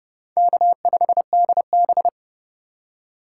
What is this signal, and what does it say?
Morse code, heard as a keyed tone, about 31 wpm, 700 Hz. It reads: K5DB
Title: USA Call Signs - 4 Characters - 3x Character Spacing - 30wpm